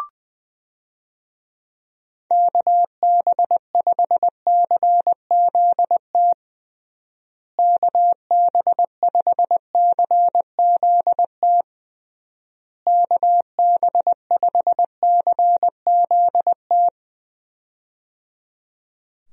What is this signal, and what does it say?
Morse code, heard as a keyed tone, about 21 wpm, 700 Hz. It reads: KB5CZT KB5CZT KB5CZT
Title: USA Call Signs - 6 Characters - Repeated 3x - 20wpm